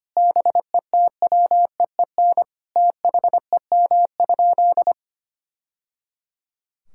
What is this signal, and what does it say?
Morse code, heard as a keyed tone, about 25 wpm, 700 Hz. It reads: BETWEEN THEM?
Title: Sentences from Top 1000 Words - Review Part 6 - 25wpm